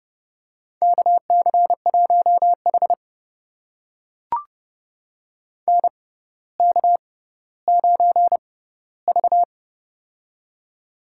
Morse code, heard as a keyed tone, about 32 wpm, spaced as for 16 wpm, 700 Hz. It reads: KC1H NK9V
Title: USA Call Signs - 4 Characters - 6x Character Spacing - 30wpm